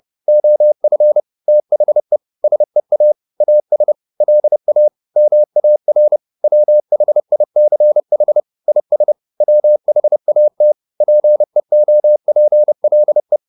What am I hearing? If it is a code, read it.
Morse code, heard as a keyed tone, about 30 wpm, 600 Hz. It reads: OF THE SEA AS LA MAR WHICH IS WHAT PEOPLE
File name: the old man and the sea / the old man and the sea 1 30wpm